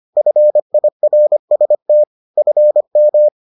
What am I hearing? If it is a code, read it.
Morse code, heard as a keyed tone, about 25 wpm, 600 Hz. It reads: FIRST FM